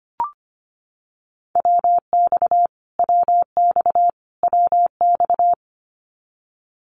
Morse code, heard as a keyed tone, about 25 wpm, 700 Hz. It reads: WX WX WX
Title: Abbreviations Repeated 3x - 25wpm